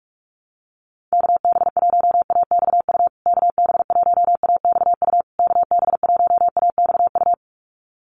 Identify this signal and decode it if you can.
Morse code, heard as a keyed tone, about 44 wpm, 690 Hz. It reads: KB1AXU KB1AXU KB1AXU
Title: USA Call Signs Repeated 3x 45wpm